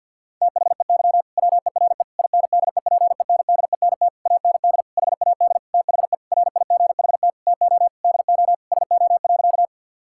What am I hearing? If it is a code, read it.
Morse code, heard as a keyed tone, about 50 wpm, 700 Hz. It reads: THEY WERE INDEPENDENT AND HAD THE RIGHT TO DO SO.
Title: Sentences from Top 1000 Words - Review Part 6 - 50wpm